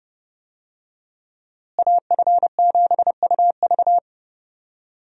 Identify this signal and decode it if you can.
Morse code, heard as a keyed tone, about 31 wpm, 700 Hz. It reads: AF7UV
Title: USA Call Signs - 5 Characters - Repeated 2x - 30wpm